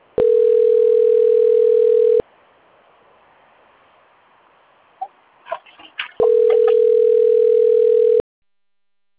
Ringback tone (North America), 440 + 480 Hz.
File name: rg-1600-unknown- - - .77513